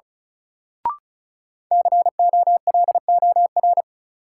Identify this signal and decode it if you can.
Morse code, heard as a keyed tone, about 36 wpm, 700 Hz. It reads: COLOR
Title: Sentences from Top 1600 Words - New Word Drill Alt - 35wpm